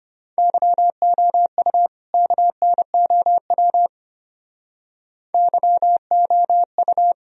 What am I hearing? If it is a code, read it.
Morse code, heard as a keyed tone, about 32 wpm, 700 Hz. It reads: YOU KNOW YOU